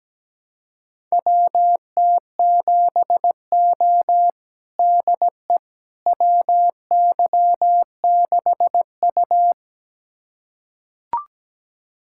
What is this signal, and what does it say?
Morse code, heard as a keyed tone, about 17 wpm, 700 Hz. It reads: WT7O DE WY6U
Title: USA Callsign DE Callsign - 4 Characters - 17wpm